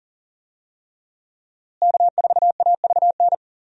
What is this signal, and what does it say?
Morse code, heard as a keyed tone, about 40 wpm, 700 Hz. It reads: K4AVN
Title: USA Call Signs - 5 Characters - Repeated 3x - 40wpm